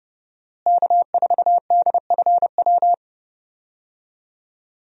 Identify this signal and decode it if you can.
Morse code, heard as a keyed tone, about 30 wpm, 700 Hz. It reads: K4DFW
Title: USA Call Signs - 5 Characters - Repeated 2x - 30wpm